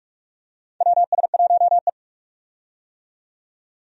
Morse code, heard as a keyed tone, about 46 wpm, 700 Hz. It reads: WS1E
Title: USA Call Signs - 4 Characters - 45wpm